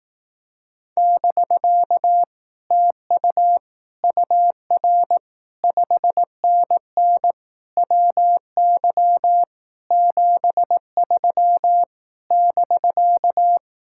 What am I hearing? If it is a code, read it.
Morse code, heard as a keyed tone, about 18 wpm, 700 Hz. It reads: <BK> TU UR 5NN WY 73 <BK>
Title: Speed-Racing Common QSO Phrases Encore - 15wpm